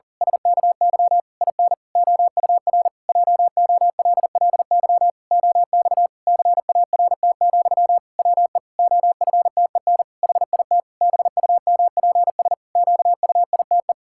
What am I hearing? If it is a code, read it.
Morse code, heard as a keyed tone, about 40 wpm, 700 Hz. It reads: SKY IN OUR JOLLY OX CART, WE OFTEN HIT BUMPS QUITE
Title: Pangram Challenge v2 - 40wpm